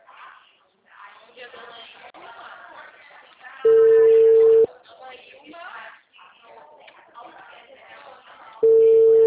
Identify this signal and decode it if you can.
Ringback tone (China, 450 Hz), with speech.